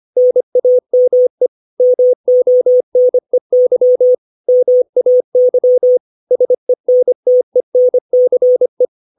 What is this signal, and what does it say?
Morse code, heard as a keyed tone, about 25 wpm, 500 Hz. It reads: NAME MONEY MAY SENTENCE
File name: 632 words / 632 Words 25WPM